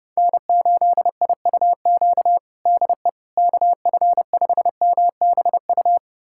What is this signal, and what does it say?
Morse code, heard as a keyed tone, about 32 wpm, 700 Hz. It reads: N8IUQ DE KF5MBU
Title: USA Callsign DE Callsign 30wpm